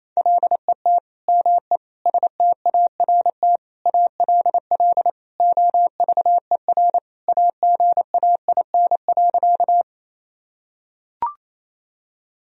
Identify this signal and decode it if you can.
Morse code, heard as a keyed tone, about 28 wpm, 700 Hz. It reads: LET ME START ALL OVER AGAIN.
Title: Sentences from Top 1000 Words - Review Part 1 - 28wpm